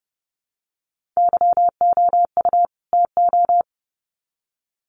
Morse code, heard as a keyed tone, about 29 wpm, 700 Hz. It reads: YOU TO
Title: Sets of 2 Words 30wpm